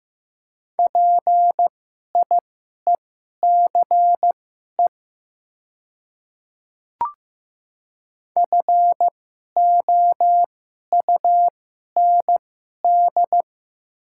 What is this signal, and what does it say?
Morse code, heard as a keyed tone, about 15 wpm, spaced as for 11 wpm, 700 Hz. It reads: PIECE FOUND
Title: 2x Character Spacing - Top 500 Words - 15wpm